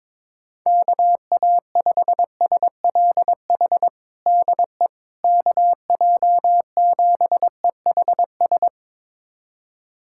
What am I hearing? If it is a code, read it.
Morse code, heard as a keyed tone, about 21 wpm, 700 Hz. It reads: KA5SLH DE KJ7EHS
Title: USA Callsign DE Callsign 22wpm